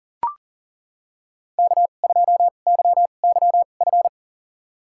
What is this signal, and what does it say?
Morse code, heard as a keyed tone, about 40 wpm, spaced as for 29 wpm, 700 Hz. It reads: K2YYF
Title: USA Call Signs - 5 Characters - 2x Character Spacing - 40wpm